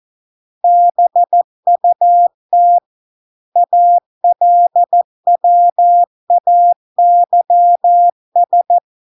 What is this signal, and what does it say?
Morse code, heard as a keyed tone, about 14 wpm, 700 Hz. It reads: BUT ALWAYS